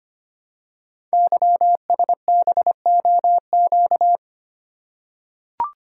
Morse code, heard as a keyed tone, about 25 wpm, 700 Hz. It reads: YSBOQ
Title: Code Groups - 5 Letters - 25wpm